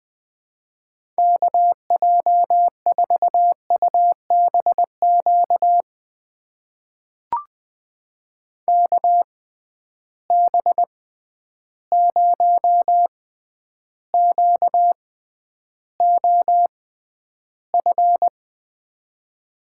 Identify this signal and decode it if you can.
Morse code, heard as a keyed tone, about 20 wpm, spaced as for 10 wpm, 700 Hz. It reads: KJ4UBQ KB0QOF
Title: USA Call Signs - 6 Characters - 6x Character Spacing - 20wpm